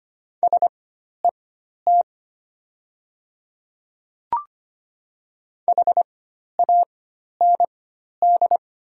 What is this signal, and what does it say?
Morse code, heard as a keyed tone, about 25 wpm, spaced as for 12 wpm, 700 Hz. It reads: SET HAND
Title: 4x Character Spacing - Top 300 Words - 25wpm